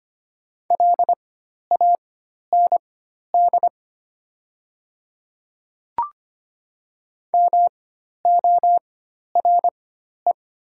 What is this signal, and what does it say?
Morse code, heard as a keyed tone, about 25 wpm, spaced as for 12 wpm, 700 Hz. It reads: LAND MORE
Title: 4x Character Spacing - Top 200 Words - 25wpm